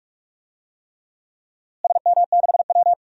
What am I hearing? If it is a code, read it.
Morse code, heard as a keyed tone, about 44 wpm, 700 Hz. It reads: SM6W
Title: DXCC-100 Call Signs - 4 Characters - Repeated 3x - 45wpm